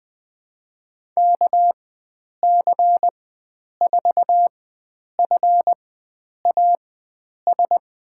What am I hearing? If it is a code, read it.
Morse code, heard as a keyed tone, about 20 wpm, spaced as for 9 wpm, 700 Hz. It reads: KC4FAS
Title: USA Call Signs - 4x Character Spacing - 20wpm